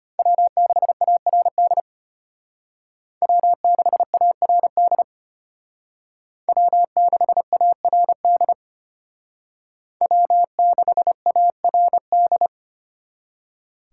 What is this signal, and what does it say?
Morse code, heard as a keyed tone, about 30 wpm, 700 Hz. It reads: W6ARD W6ARD W6ARD W6ARD
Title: Speed Racing - USA 1x3 Call Signs 25wpm